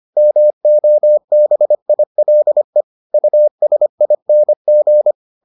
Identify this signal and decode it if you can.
Morse code, heard as a keyed tone, about 25 wpm, 600 Hz. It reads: MOBILE USING